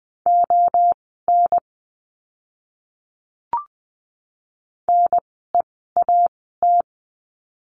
Morse code, heard as a keyed tone, about 20 wpm, spaced as for 14 wpm, 700 Hz. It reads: ON NEAT
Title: 061 Lesson 11f - Letter h - Words 20wpm